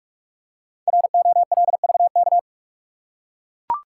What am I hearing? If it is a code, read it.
Morse code, heard as a keyed tone, about 44 wpm, 700 Hz. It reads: ROLVK
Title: Code Groups - 5 Letter:Numbers - 45wpm